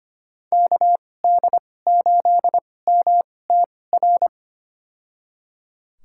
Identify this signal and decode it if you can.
Morse code, heard as a keyed tone, about 25 wpm, spaced as for 18 wpm, 700 Hz. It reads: KD8MTR
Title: USA Call Signs - 2x Character Spacing - 25wpm